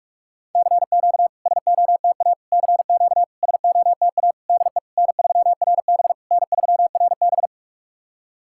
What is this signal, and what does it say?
Morse code, heard as a keyed tone, about 44 wpm, 700 Hz. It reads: CQ SOTA CQ SOTA DE N3RB N3RB
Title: SOTA QSO Practice 45wpm